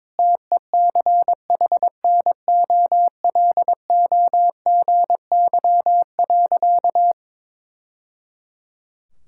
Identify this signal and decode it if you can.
Morse code, heard as a keyed tone, about 22 wpm, 700 Hz. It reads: TECHNOLOGY.
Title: Sentences from Top 1000 Words - Review Part 5 - 22wpm